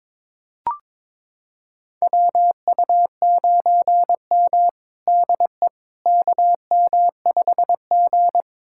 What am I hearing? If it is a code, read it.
Morse code, heard as a keyed tone, about 22 wpm, 700 Hz. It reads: WU9M DE KM5G
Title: USA Callsign DE Callsign - 4 Characters - 22wpm